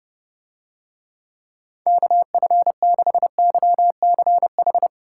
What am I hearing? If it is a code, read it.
Morse code, heard as a keyed tone, about 30 wpm, 700 Hz. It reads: KF6YCH